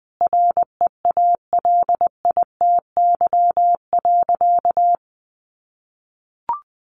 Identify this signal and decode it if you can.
Morse code, heard as a keyed tone, about 20 wpm, 700 Hz. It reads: REALITY.
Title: Sentences from Top 1500 Words 20wpm